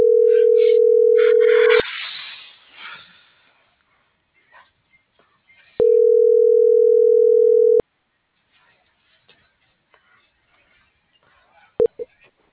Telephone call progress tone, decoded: ringback tone

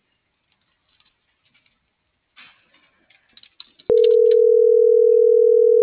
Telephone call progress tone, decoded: ringback tone

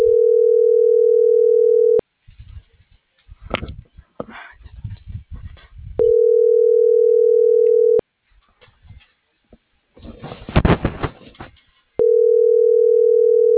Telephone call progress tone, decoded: ringback tone

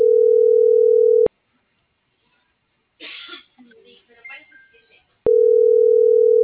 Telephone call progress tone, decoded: ringback tone

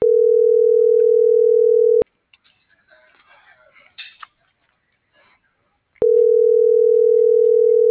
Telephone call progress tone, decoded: ringback tone

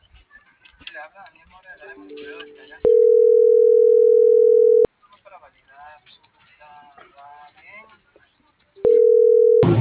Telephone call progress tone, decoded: ringback tone

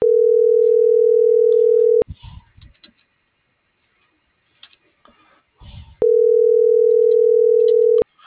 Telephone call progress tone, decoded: ringback tone

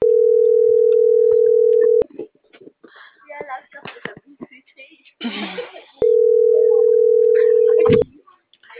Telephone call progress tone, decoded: ringback tone

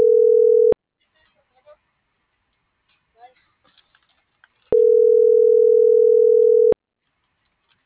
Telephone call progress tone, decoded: ringback tone

